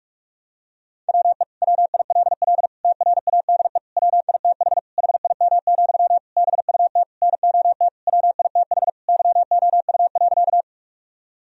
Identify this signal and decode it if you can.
Morse code, heard as a keyed tone, about 46 wpm, 700 Hz. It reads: WE WILL TRADE WITH HIM, BUT NOT WITH YOU.
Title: Sentences from Top 1000 Words - Review Part 4 - 45wpm